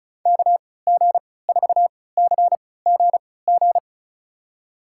Morse code, heard as a keyed tone, about 36 wpm, spaced as for 20 wpm, 700 Hz. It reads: KG4CGG
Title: USA Call Signs - 3x Character Spacing - 35wpm